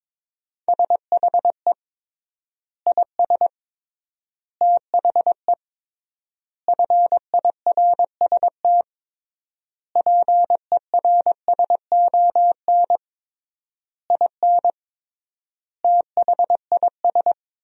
Morse code, heard as a keyed tone, about 22 wpm, 700 Hz. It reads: SHE IS THE FIRST PERSON IN THIS
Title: 3x Word-Spacing - Sentences from Top 400 Words - 22wpm